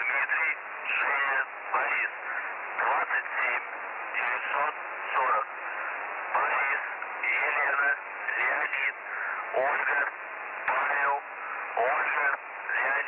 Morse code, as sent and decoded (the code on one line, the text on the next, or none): none